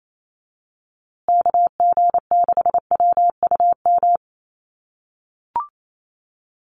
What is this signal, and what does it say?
Morse code, heard as a keyed tone, about 28 wpm, 700 Hz. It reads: KG6WUM
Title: USA Call Signs Repeated 3x 28wpm